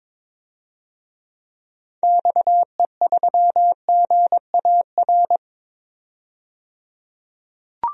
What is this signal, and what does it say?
Morse code, heard as a keyed tone, about 21 wpm, 700 Hz. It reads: XE3GAR
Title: DXCC-100 Call Signs - 6 Characters - 22wpm